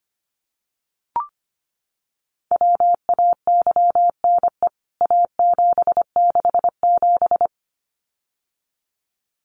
Morse code, heard as a keyed tone, about 25 wpm, 700 Hz. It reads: WAYNE A767
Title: CWT Exchange 25wpm